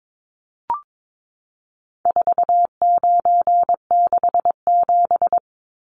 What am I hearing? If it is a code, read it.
Morse code, heard as a keyed tone, about 22 wpm, 700 Hz. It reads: 4967
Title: Numbers - 4-Digit 22wpm